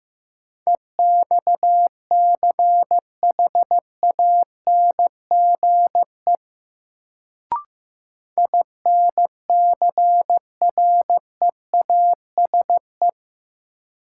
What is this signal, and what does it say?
Morse code, heard as a keyed tone, about 15 wpm, 700 Hz. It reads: EXCHANGE INCREASE
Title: Sets of 5 Words - New Word Drill - 15wpm